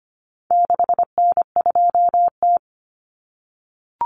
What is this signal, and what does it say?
Morse code, heard as a keyed tone, about 25 wpm, 700 Hz. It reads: 6N2T
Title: Common Cut Numbers - 4-Digit 25wpm